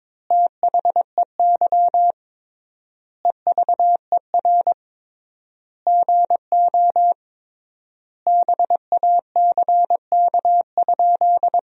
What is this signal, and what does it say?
Morse code, heard as a keyed tone, about 22 wpm, 700 Hz. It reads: THEY EVER GO BACK?
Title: 3x Word-Spacing - Sentences from Top 400 Words - 22wpm